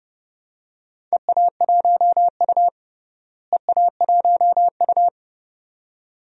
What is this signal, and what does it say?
Morse code, heard as a keyed tone, about 30 wpm, 700 Hz. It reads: EA1U EA1U
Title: DXCC-100 Call Signs - 4 Characters - Repeated 2x - 30wpm